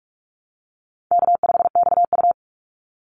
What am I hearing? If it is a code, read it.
Morse code, heard as a keyed tone, about 46 wpm, 700 Hz. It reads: K5XU
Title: Easy Call Signs - 45wpm